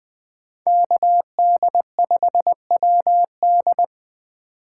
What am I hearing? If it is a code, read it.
Morse code, heard as a keyed tone, about 20 wpm, 700 Hz. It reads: KD5WD
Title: Speed Racing - USA 2x2 Call Signs 17wpm